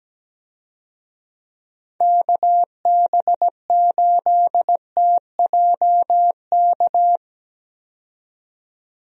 Morse code, heard as a keyed tone, about 17 wpm, 700 Hz. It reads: KB8TJK
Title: USA Call Signs - 6 Characters - 17wpm